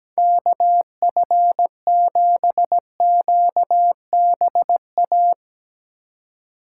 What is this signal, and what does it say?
Morse code, heard as a keyed tone, about 17 wpm, 700 Hz. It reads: KF7QBA
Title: USA Call Signs - 6 Characters - 4x Character Spacing - 17wpm